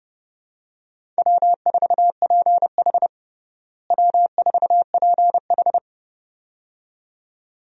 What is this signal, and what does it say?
Morse code, heard as a keyed tone, about 31 wpm, 700 Hz. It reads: W4PH W4PH
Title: USA Call Signs - 4 Characters - Repeated 2x - 30wpm